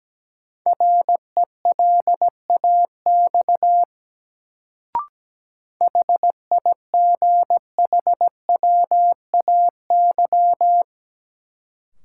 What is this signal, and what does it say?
Morse code, heard as a keyed tone, about 17 wpm, 700 Hz. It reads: RELAX HIGHWAY
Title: Sentences from Top 1400 Words - New Word Drill - 17wpm